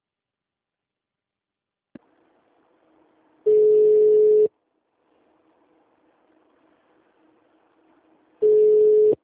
Ringback tone (Europe, 425 Hz).